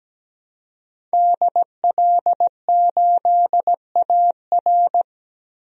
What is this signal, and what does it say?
Morse code, heard as a keyed tone, about 17 wpm, 700 Hz. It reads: DL8AR
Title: DXCC-100 Call Signs - 5 Characters - Repeated 3x - 17wpm